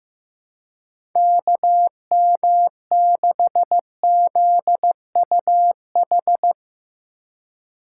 Morse code, heard as a keyed tone, about 15 wpm, 700 Hz. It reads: KM6ZUH
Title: USA Call Signs - 6 Characters - 2x Character Spacing - 15wpm